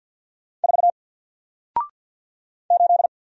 Morse code, heard as a keyed tone, about 48 wpm, 700 Hz. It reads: 4 8
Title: Single Letter-Number Alt 50wpm